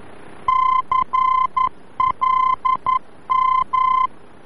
Morse code, heard as a keyed tone, about 11 wpm, 1030 Hz. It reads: CLM
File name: france / mx / CLM-VOR-F